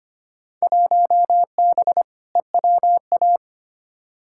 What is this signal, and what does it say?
Morse code, heard as a keyed tone, about 25 wpm, 700 Hz. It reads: 1B EWA
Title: Field Day Exchanges Encore - 25wpm